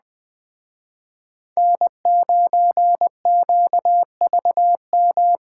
Morse code, heard as a keyed tone, about 20 wpm, 700 Hz. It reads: N9QVM
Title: Speed Racing - USA Call Signs 17wpm